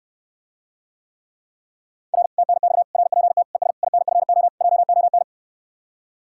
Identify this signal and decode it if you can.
Morse code, heard as a keyed tone, about 69 wpm, 700 Hz. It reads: R TNX RPT ES INFO JON